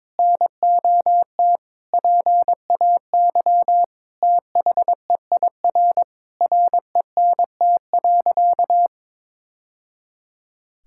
Morse code, heard as a keyed tone, about 22 wpm, 700 Hz. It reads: NOT PAY THEIR RENT.
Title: Sentences from Top 1000 Words - Review Part 5 - 22wpm